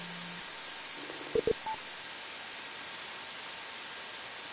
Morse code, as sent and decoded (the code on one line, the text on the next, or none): none